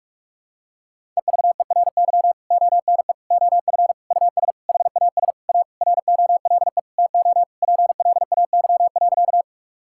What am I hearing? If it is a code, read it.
Morse code, heard as a keyed tone, about 45 wpm, 700 Hz. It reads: EVERY ONE OF US HAS A ROLE TO PLAY.
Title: Sentences from Top 1000 Words - Review Part 6 - 45wpm